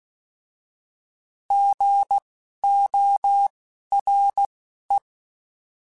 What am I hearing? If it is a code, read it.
Morse code, heard as a keyed tone, about 16 wpm, spaced as for 11 wpm, 800 Hz. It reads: GORE
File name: Alphabetical / Alphabetical mp3s / Gs @10